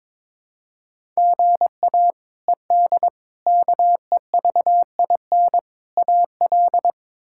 Morse code, heard as a keyed tone, about 22 wpm, 700 Hz. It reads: GA ED KEVIN AL
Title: K1USN SST QSO Practice - 22wpm